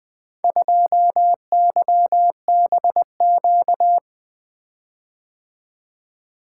Morse code, heard as a keyed tone, about 20 wpm, 700 Hz. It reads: 2YBQ
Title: Four Letter-Number Combinations - 20wpm